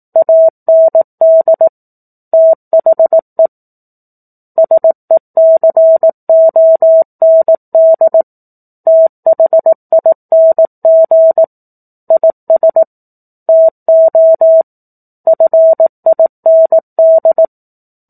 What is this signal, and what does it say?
Morse code, heard as a keyed tone, about 18 wpm, 650 Hz. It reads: AND THE SECOND THING IS TO FIND
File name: 18 WPM